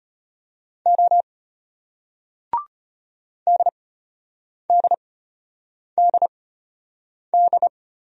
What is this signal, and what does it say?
Morse code, heard as a keyed tone, about 33 wpm, spaced as for 8 wpm, 700 Hz. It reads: O DDDD